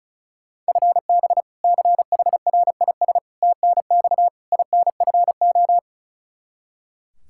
Morse code, heard as a keyed tone, about 34 wpm, 700 Hz. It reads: FB CHRIS TNX INFO
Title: Practice QSO - Exchange 3 - More Info - 35wpm